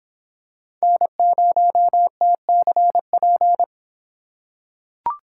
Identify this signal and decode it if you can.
Morse code, heard as a keyed tone, about 23 wpm, 700 Hz. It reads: N0TCP